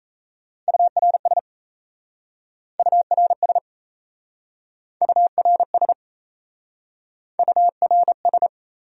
Morse code, heard as a keyed tone, about 36 wpm, 700 Hz. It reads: URS URS URS URS